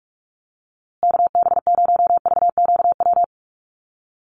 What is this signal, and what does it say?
Morse code, heard as a keyed tone, about 44 wpm, 700 Hz. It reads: KB0VQW
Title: USA Call Signs Repeated 2x 45wpm